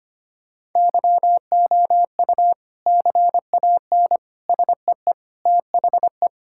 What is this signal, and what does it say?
Morse code, heard as a keyed tone, about 25 wpm, 700 Hz. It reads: YOU CAN SEE THE
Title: Sentences from Top 1600 Words 25wpm